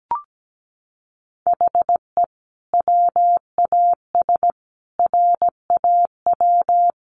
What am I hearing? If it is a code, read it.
Morse code, heard as a keyed tone, about 17 wpm, 700 Hz. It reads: HE WAS RAW
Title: Sentences from Top 1500 Words 17wpm